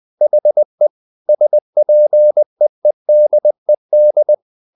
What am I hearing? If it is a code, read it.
Morse code, heard as a keyed tone, about 20 wpm, 600 Hz. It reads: HE SPEEDED
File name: the old man and the sea part 2 20wpm